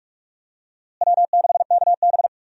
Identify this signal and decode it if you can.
Morse code, heard as a keyed tone, about 46 wpm, 700 Hz. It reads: W6KB
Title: USA Call Signs - 4 Characters - Repeated 3x - 45wpm